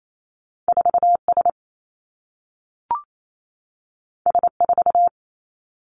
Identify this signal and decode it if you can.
Morse code, heard as a keyed tone, about 28 wpm, 700 Hz. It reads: 4 S S 4
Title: Two Challenging Letter-Number Combinations 28wpm